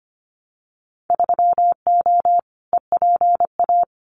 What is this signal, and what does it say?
Morse code, heard as a keyed tone, about 25 wpm, 700 Hz. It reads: 3O EPA
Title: Winter Field Day Exchanges Encore 25wpm